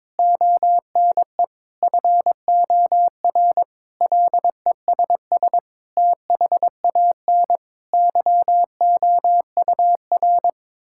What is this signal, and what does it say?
Morse code, heard as a keyed tone, about 22 wpm, 700 Hz. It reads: ONE FOR LESS THAN YOUR
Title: Sentences from Top 1000 Words - Review Part 5 - 22wpm